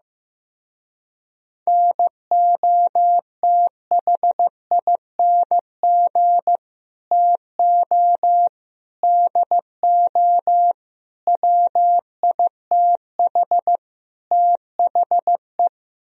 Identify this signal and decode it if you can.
Morse code, heard as a keyed tone, about 15 wpm, 700 Hz. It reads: NOTHING TO DO WITH THE
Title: Speed-Racing - Sets of 5 Words 15wpm